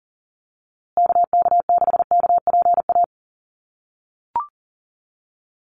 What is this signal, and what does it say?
Morse code, heard as a keyed tone, about 40 wpm, 700 Hz. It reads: KK6KPA
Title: USA Call Signs Repeated 3x 40wpm